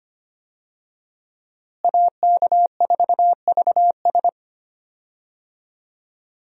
Morse code, heard as a keyed tone, about 25 wpm, 700 Hz. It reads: AK4VS
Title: USA Call Signs - 5 Characters - Repeated 3x - 25wpm